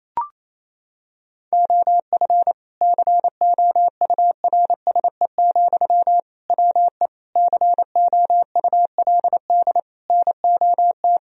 Morse code, heard as a keyed tone, about 28 wpm, 700 Hz. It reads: OF COURSE, WE COULD NOT
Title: Sentences from Top 1000 Words - Review Part 2 - 28wpm